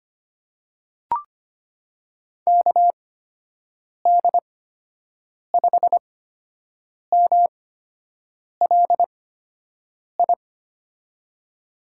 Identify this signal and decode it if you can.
Morse code, heard as a keyed tone, about 25 wpm, spaced as for 7 wpm, 700 Hz. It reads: KD5MLI
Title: USA Call Signs - 6 Characters - 8x Character Spacing - 25wpm